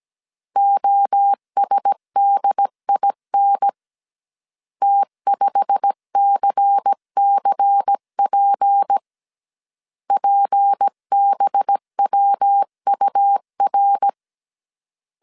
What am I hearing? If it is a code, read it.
Morse code, heard as a keyed tone, about 17 wpm, 800 Hz. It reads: OSDIN T5CCP PBWUR
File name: CW / DET / 1 9 1500C S B 16wpm 2856